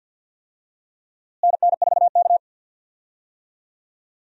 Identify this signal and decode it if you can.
Morse code, heard as a keyed tone, about 47 wpm, 700 Hz. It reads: NN4K
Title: USA Call Signs - 4 Characters - Repeated 3x - 50wpm